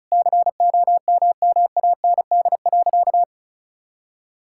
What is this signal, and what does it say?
Morse code, heard as a keyed tone, about 36 wpm, 700 Hz. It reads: COMMAND.
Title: Sentences from Top 1600 Words 35wpm